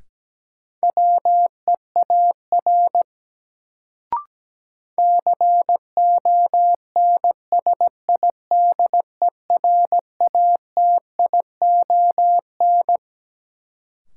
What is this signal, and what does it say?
Morse code, heard as a keyed tone, about 17 wpm, 700 Hz. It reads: WEAR CONSIDERATION
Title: Sentences from Top 1500 Words - New Word Drill Alt - 17wpm